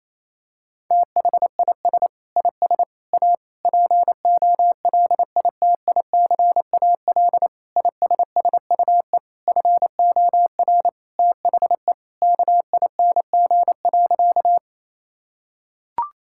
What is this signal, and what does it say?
Morse code, heard as a keyed tone, about 28 wpm, 700 Hz. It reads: THIS IS A POLITICAL ISSUE FOR THE KING.
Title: Sentences from Top 1000 Words - Review Part 6 - 28wpm